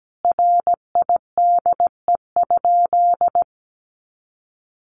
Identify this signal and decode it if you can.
Morse code, heard as a keyed tone, about 17 wpm, 700 Hz. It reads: RIDE?
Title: Sentences from Top 1200 Words 17wpm